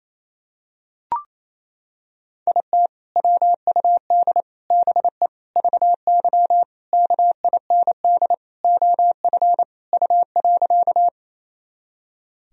Morse code, heard as a keyed tone, about 28 wpm, 700 Hz. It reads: IT WUD BE VY KIND OF U.
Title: Sentences from Top 200 Words and Abbreviations 28wpm